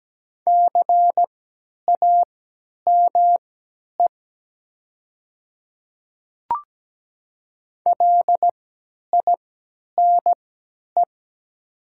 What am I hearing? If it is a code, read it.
Morse code, heard as a keyed tone, about 17 wpm, spaced as for 10 wpm, 700 Hz. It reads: CAME LINE
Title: 3x Character Spacing - Top 300 Words - 17wpm